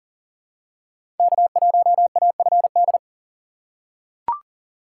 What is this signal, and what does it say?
Morse code, heard as a keyed tone, about 37 wpm, 700 Hz. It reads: K1AFD
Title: USA Call Signs - 4x Character Spacing - 40wpm